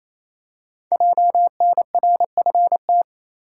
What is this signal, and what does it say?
Morse code, heard as a keyed tone, about 28 wpm, 700 Hz. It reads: JNRFT
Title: Code Groups - 5 Letter:Numbers - 28wpm